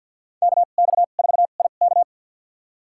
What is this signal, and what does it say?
Morse code, heard as a keyed tone, about 48 wpm, spaced as for 35 wpm, 700 Hz. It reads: KX4IK
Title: USA Call Signs - 5 Characters - 2x Character Spacing - 50wpm